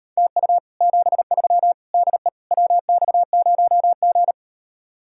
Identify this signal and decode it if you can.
Morse code, heard as a keyed tone, about 38 wpm, 700 Hz. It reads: TU 73 DE WX0G